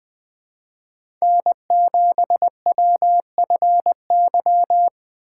Morse code, heard as a keyed tone, about 21 wpm, 700 Hz. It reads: N7WFY